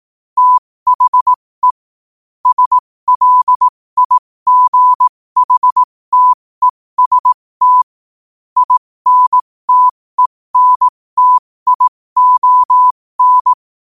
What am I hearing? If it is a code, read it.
Morse code, heard as a keyed tone, about 18 wpm, spaced as for 15 wpm, 1000 Hz. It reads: THE SLIGHTEST INTENTION
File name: qotdfiles / 15wpm